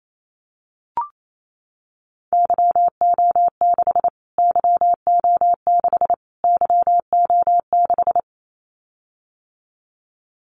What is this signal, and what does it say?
Morse code, heard as a keyed tone, about 28 wpm, 700 Hz. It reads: YO6 YO6 YO6
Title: Callsign Prefixes for Top 25 DXCC - Repeated 3x - 28wpm